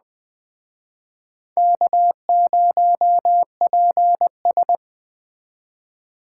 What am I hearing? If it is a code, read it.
Morse code, heard as a keyed tone, about 20 wpm, 700 Hz. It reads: K0PS
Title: Speed Racing - USA 1x2 Call Signs 17wpm